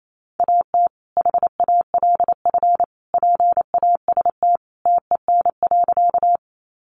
Morse code, heard as a keyed tone, about 28 wpm, 700 Hz. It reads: AT HALF PAST TEN.
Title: Sentences from Top 600 Words 28wpm